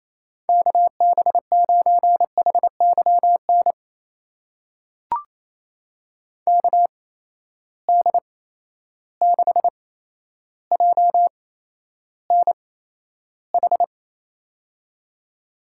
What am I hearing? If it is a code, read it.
Morse code, heard as a keyed tone, about 28 wpm, spaced as for 12 wpm, 700 Hz. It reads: KB9HYN KD6JNH
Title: USA Call Signs - 6 Characters - 8x Character Spacing - 28wpm